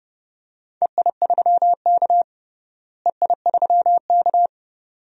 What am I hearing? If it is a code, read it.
Morse code, heard as a keyed tone, about 32 wpm, 700 Hz. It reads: EI3K EI3K
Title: DXCC-100 Call Signs - 4 Characters - Repeated 2x - 30wpm